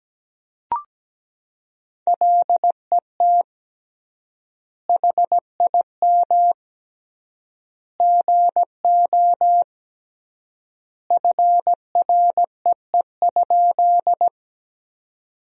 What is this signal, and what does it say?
Morse code, heard as a keyed tone, about 17 wpm, 700 Hz. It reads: LET HIM GO FREE?
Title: 3x Word-Spacing - Sentences from Top 500 Words - 17wpm